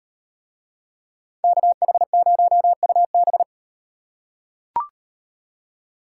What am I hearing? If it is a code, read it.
Morse code, heard as a keyed tone, about 38 wpm, 700 Hz. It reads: KH0UB